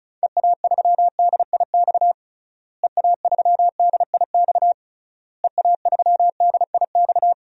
Morse code, heard as a keyed tone, about 34 wpm, 700 Hz. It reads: EA3DIX EA3DIX EA3DIX
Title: DXCC-100 Call Signs - 6 Characters - Repeated 3x - 35wpm